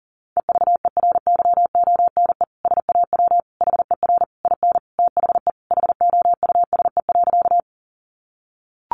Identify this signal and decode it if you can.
Morse code, heard as a keyed tone, about 40 wpm, 690 Hz. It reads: EVERYONE SAW HER IN THE HOUSE.
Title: Sentences from Top 800 Words 40wpm